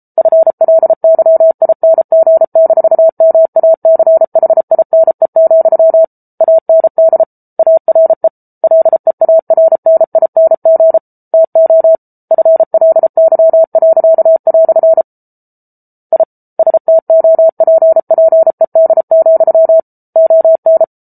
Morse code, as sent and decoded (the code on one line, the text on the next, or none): ..-. .-.. -.-- .. -. --. -....- -- .- -.-. .... .. -. --..-- / .- -. -.. / .- .-. / .-.. .- .-. -. .. -. --. / - --- / ..-. .-.. -.-- .-.-.- .-..-. / .. / ... - --- .--. .--. -.. --..-- / --- -.
FLYING-MACHINE, AND ARE LEARNING TO FLY." I STOPPED, ON